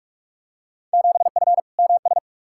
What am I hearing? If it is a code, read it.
Morse code, heard as a keyed tone, about 41 wpm, 700 Hz. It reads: 7F MS
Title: Field Day Exchanges Encore - 45wpm